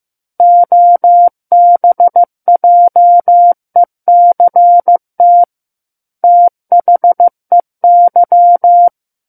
Morse code, heard as a keyed tone, about 15 wpm, 700 Hz. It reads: OBJECT THEY
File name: Book One / 06-Chapter 6 - 15 WPM